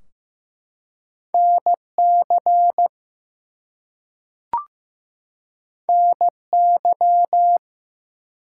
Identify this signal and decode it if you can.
Morse code, heard as a keyed tone, about 15 wpm, 700 Hz. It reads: NC NY
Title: US States and Canadian Provinces 15wpm